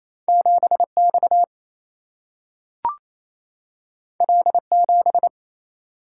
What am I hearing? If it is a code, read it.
Morse code, heard as a keyed tone, about 28 wpm, 700 Hz. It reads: 7 X L 7
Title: Single-Letter Single-Number 28wpm